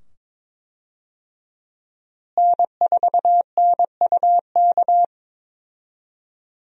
Morse code, heard as a keyed tone, about 22 wpm, 700 Hz. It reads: N4NUK
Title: USA Call Signs - 5 Characters - Repeated 2x - 22wpm